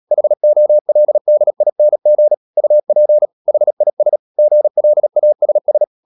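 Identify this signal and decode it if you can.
Morse code, heard as a keyed tone, about 37 wpm, 600 Hz. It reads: HOLDING UP HIS GLASS
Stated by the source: Morse code practice at 35–39 wpm sending English text of Ken Follett's novel Never, from CW med Gnister.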